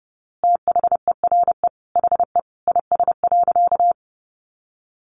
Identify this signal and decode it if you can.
Morse code, heard as a keyed tone, about 30 wpm, 700 Hz. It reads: THERE HE IS.
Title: A Sound Of Thunder 30wpm